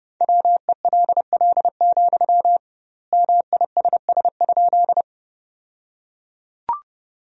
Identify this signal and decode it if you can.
Morse code, heard as a keyed tone, about 30 wpm, 700 Hz. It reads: WELL, MISS?
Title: 2x Word-Spacing - Sentences from Top 500 Words - 30wpm